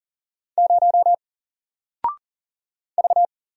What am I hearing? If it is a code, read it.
Morse code, heard as a keyed tone, about 40 wpm, 700 Hz. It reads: 0 V
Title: Single Character (Letter:Number:Punctuation:Prosign) 40wpm